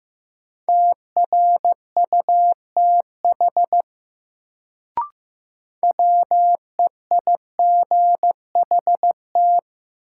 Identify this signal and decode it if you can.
Morse code, heard as a keyed tone, about 15 wpm, 700 Hz. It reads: TRUTH WEIGHT
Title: Sentences from Top 800 Words - New Word Drill - 15wpm